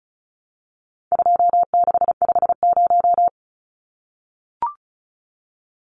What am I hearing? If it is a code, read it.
Morse code, heard as a keyed tone, about 34 wpm, 700 Hz. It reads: 2650
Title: Numbers - 4-Digit 35wpm